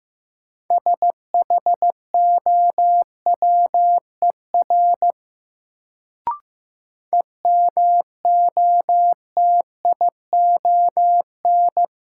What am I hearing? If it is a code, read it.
Morse code, heard as a keyed tone, about 16 wpm, 700 Hz. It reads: SHOWER EMOTION
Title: Sentences from Top 1600 Words - New Word Drill Alt - 15wpm